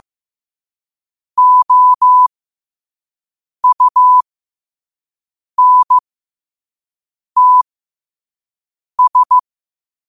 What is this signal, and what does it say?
Morse code, heard as a keyed tone, about 15 wpm, spaced as for 5 wpm, 1000 Hz. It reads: OUNTS
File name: qotdfiles / 5wpm